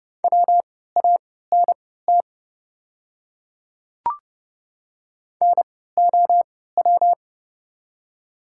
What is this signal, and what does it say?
Morse code, heard as a keyed tone, about 30 wpm, spaced as for 17 wpm, 700 Hz. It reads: WANT NOW
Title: 3x Character Spacing - Top 300 Words - 30wpm